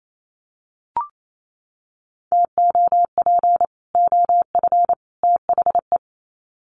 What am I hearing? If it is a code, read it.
Morse code, heard as a keyed tone, about 28 wpm, 700 Hz. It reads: TOP OF THE
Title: Sets of 3 Words 28wpm